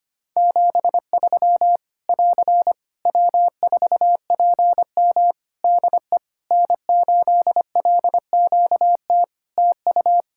Morse code, heard as a keyed tone, about 25 wpm, 700 Hz. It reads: 73 <AR> W4PM DE N8LQT TU
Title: Practice QSO - Exchange 4 - Ending - 25wpm